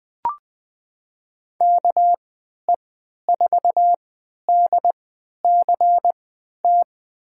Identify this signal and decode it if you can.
Morse code, heard as a keyed tone, about 20 wpm, spaced as for 11 wpm, 700 Hz. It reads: KE4DCT
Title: USA Call Signs - 6 Characters - 3x Character Spacing - 20wpm